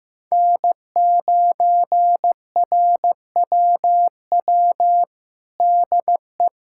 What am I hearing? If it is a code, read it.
Morse code, heard as a keyed tone, about 15 wpm, 700 Hz. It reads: N9RWW DE
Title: USA Callsign DE Callsign - 5 Characters - 15wpm